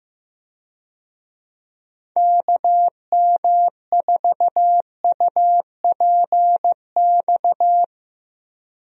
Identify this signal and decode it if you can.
Morse code, heard as a keyed tone, about 15 wpm, 700 Hz. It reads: KM4UPX
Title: USA Call Signs - 6 Characters - Repeated 2x - 15wpm